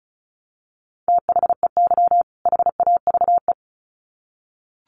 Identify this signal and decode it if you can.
Morse code, heard as a keyed tone, about 34 wpm, 700 Hz. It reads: THEY HAVE
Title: Sets of 2 Words 35wpm